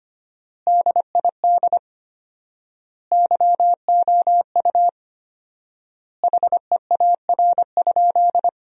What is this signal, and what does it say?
Morse code, heard as a keyed tone, about 25 wpm, 700 Hz. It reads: DID YOU HEAR?
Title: 4x Word-Spacing - Sentences from Top 300 Words - 25wpm